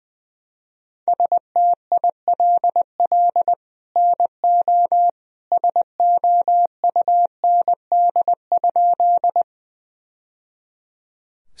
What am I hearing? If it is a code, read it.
Morse code, heard as a keyed tone, about 20 wpm, 700 Hz. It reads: STILL NO SOUND?
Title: Sentences from Top 1000 Words - Review Part 1 - 20wpm